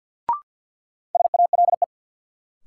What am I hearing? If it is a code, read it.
Morse code, heard as a keyed tone, about 52 wpm, 700 Hz. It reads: SALE
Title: Sentences from Top 1400 Words - New Word Drill - 50wpm